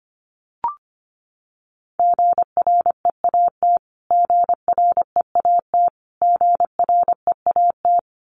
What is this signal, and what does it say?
Morse code, heard as a keyed tone, about 25 wpm, 700 Hz. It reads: GREAT GREAT GREAT
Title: Top 500 Words Repeated 25wpm